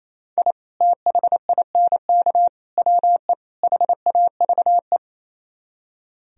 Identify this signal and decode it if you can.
Morse code, heard as a keyed tone, about 35 wpm, 700 Hz. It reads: I THINK WE HAVE
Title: Speed-Racing - Sets of 4 Words 28wpm